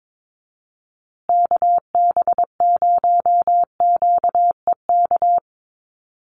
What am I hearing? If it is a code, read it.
Morse code, heard as a keyed tone, about 22 wpm, 700 Hz. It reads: KB0QEK
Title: USA Call Signs Repeated 2x 22wpm